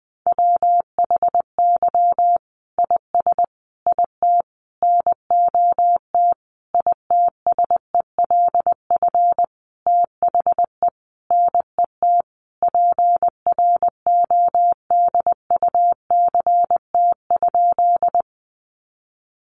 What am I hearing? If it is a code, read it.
Morse code, heard as a keyed tone, about 20 wpm, 700 Hz. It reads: WHY IS IT NOT ITSELF THE NET PRODUCT?
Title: Sentences from Top 1400 Words 20wpm